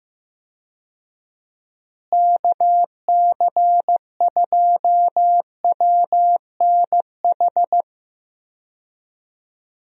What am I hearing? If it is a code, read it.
Morse code, heard as a keyed tone, about 15 wpm, 700 Hz. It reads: KC2WNH
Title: USA Call Signs - 6 Characters - Repeated 3x - 15wpm